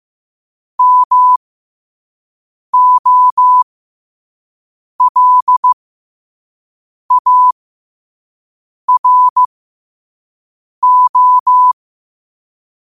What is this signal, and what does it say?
Morse code, heard as a keyed tone, about 15 wpm, spaced as for 5 wpm, 1000 Hz. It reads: MOLARO
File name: 5wpm